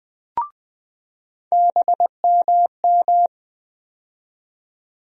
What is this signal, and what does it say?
Morse code, heard as a keyed tone, about 20 wpm, 700 Hz. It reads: BMM
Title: USA Call Sign Postfixes - 3 Letters 20wpm